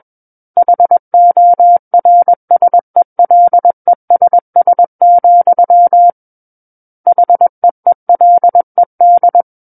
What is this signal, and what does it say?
Morse code, heard as a keyed tone, about 21 wpm, 700 Hz. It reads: HORSELESS, HEELED
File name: Book One / 12-Chapter 12 - 21 WPM